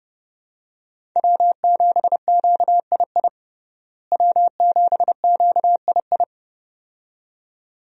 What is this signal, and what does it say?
Morse code, heard as a keyed tone, about 30 wpm, 700 Hz. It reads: W7QII W7QII
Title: USA Call Signs - 5 Characters - Repeated 2x - 30wpm